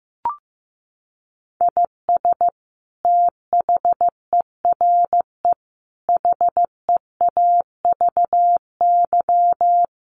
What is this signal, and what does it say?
Morse code, heard as a keyed tone, about 15 wpm, 700 Hz. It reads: IS THERE HEAVY
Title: Sentences from Top 1300 Words 15wpm